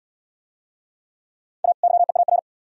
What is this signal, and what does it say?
Morse code, heard as a keyed tone, about 70 wpm, 700 Hz. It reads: A YEAR